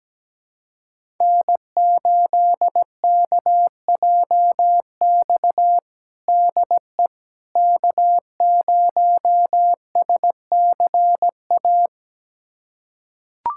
Morse code, heard as a keyed tone, about 17 wpm, 700 Hz. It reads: N8KJX DE K0SCA
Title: USA Callsign DE Callsign - 5 Characters - 17wpm